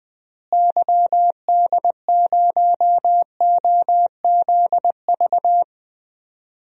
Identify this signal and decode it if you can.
Morse code, heard as a keyed tone, about 21 wpm, 700 Hz. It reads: YD0OZV
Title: DXCC-100 Call Signs - 6 Characters - Repeated 2x - 20wpm